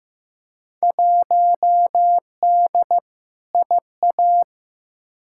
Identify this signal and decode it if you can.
Morse code, heard as a keyed tone, about 15 wpm, 700 Hz. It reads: 1D IA
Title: Field Day Exchanges - 15wpm